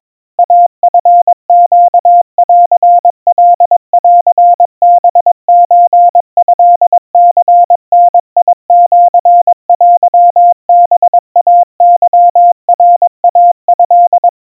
Morse code (Japanese Wabun code), heard as a keyed tone, about 22 wpm, 700 Hz. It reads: イチネンカンハソトニダシテハイケナイト
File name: kamyu 80 cpm 700 Hz Crispy